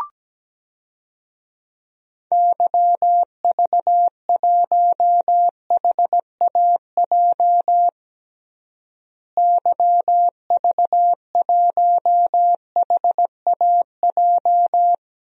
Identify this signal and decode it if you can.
Morse code, heard as a keyed tone, about 17 wpm, 700 Hz. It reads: YV1HAJ YV1HAJ
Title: DXCC-100 Call Signs - 6 Characters - Repeated 2x - 17wpm